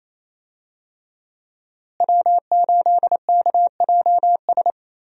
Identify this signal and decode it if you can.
Morse code, heard as a keyed tone, about 28 wpm, 700 Hz. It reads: W8KJS
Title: USA Call Signs - 5 Characters - Repeated 3x - 28wpm